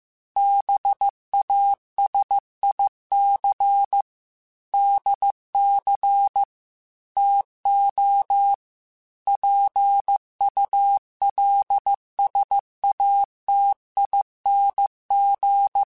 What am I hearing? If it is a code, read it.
Morse code, heard as a keyed tone, about 15 wpm, 800 Hz. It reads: BASIC DC TO PULSATING
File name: Technical Subjects / Technical Subjects-mp3s / Sw Pwr Sup @13